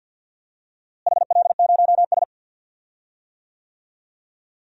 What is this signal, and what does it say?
Morse code, heard as a keyed tone, about 50 wpm, 700 Hz. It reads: HL0S
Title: DXCC-100 Call Signs - 4 Characters - Repeated 2x - 50wpm